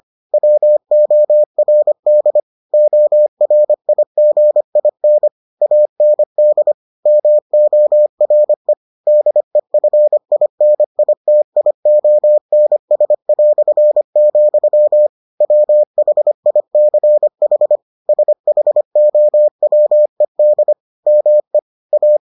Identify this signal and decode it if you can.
Morse code, heard as a keyed tone, about 25 wpm, 600 Hz. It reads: WORD ORIGIN AND MORE DEFINITIONS", WHICH SHOWED ME A